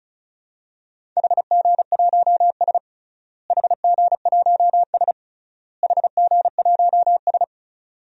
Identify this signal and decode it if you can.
Morse code, heard as a keyed tone, about 34 wpm, 700 Hz. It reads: HG1S HG1S HG1S
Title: DXCC-100 Call Signs - 4 Characters - Repeated 3x - 35wpm